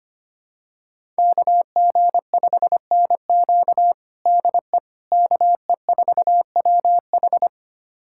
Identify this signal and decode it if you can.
Morse code, heard as a keyed tone, about 25 wpm, 700 Hz. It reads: KG5NQ DE KE4WH
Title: USA Callsign DE Callsign 25wpm